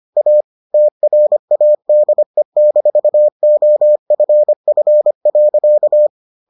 Morse code, heard as a keyed tone, about 25 wpm, 600 Hz. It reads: A TRADE-OFF.